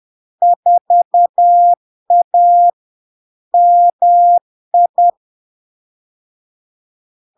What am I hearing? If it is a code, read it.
Morse code, heard as a keyed tone, about 10 wpm, 700 Hz. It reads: 4A MI
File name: LARK-FD-2022-Log-in-CW-10wpm-700-Hz